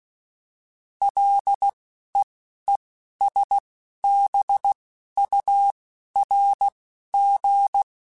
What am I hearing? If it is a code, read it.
Morse code, heard as a keyed tone, about 16 wpm, spaced as for 11 wpm, 800 Hz. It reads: LEESBURG